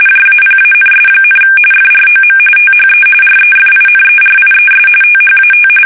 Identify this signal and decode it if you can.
RTTY, about 45.45 baud. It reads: JAPANS, FROM THE REFINERIES THAT
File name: fsk850